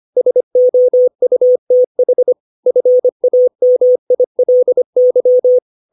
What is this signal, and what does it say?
Morse code, heard as a keyed tone, about 25 wpm, 500 Hz. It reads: SOUTH FAMILY
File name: 632 words / 632 Words 25WPM